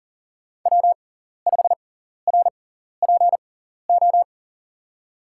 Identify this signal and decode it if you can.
Morse code, heard as a keyed tone, about 40 wpm, spaced as for 14 wpm, 700 Hz. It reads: W5RPO
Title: USA Call Signs - 5 Characters - 6x Character Spacing - 40wpm